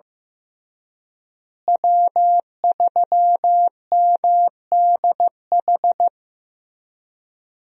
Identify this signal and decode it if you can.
Morse code, heard as a keyed tone, about 15 wpm, 700 Hz. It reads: W3MDH
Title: USA Call Signs - 5 Characters - Repeated 3x - 15wpm